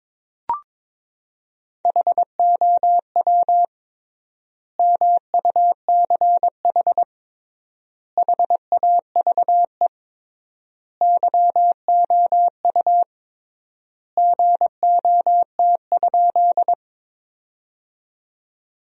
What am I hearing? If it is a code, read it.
Morse code, heard as a keyed tone, about 22 wpm, 700 Hz. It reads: HOW MUCH HAVE YOU GOT?
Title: 3x Word-Spacing - Sentences from Top 300 Words - 22wpm